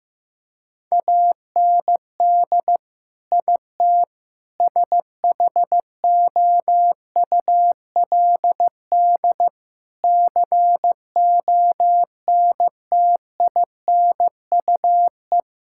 Morse code, heard as a keyed tone, about 15 wpm, 700 Hz. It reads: AND IT SHOULD CONTINUE
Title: Sentences from Top 1000 Words - Review Part 6 - 15wpm